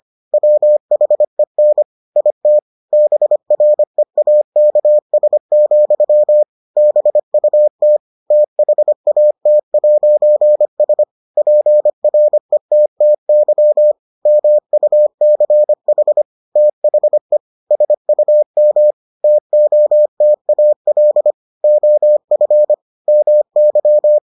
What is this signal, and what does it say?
Morse code, heard as a keyed tone, about 25 wpm, 600 Hz. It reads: WHEN IT BREAKS, BUT THAT'S PRETTY MUCH THE SUM TOTAL OF MY